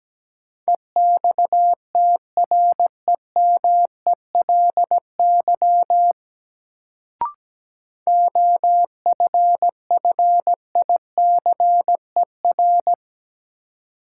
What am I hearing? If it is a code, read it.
Morse code, heard as a keyed tone, about 17 wpm, 700 Hz. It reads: EXTREMELY OFFICER
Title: Sentences from Top 1200 Words - New Word Drill Alt - 17wpm